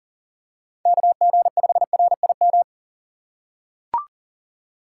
Morse code, heard as a keyed tone, about 40 wpm, 700 Hz. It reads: KG5RIM
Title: USA Call Signs - 6 Characters - 4x Character Spacing - 40wpm